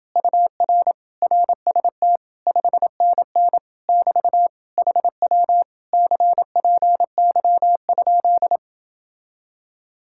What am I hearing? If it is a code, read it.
Morse code, heard as a keyed tone, about 29 wpm, 700 Hz. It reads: UR RST 5NN <BT> HW CPY?